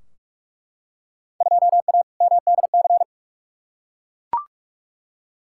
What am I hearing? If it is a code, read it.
Morse code, heard as a keyed tone, about 42 wpm, 700 Hz. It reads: 2A MDC
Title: Field Day Exchanges Encore - 45wpm